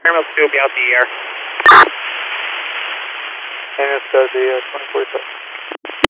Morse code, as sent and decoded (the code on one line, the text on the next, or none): none